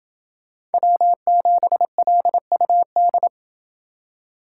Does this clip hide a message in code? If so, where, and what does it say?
Yes: Morse code from 0.7 to 3.3 s, heard as a keyed tone, about 28 wpm, 700 Hz. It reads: W7LUD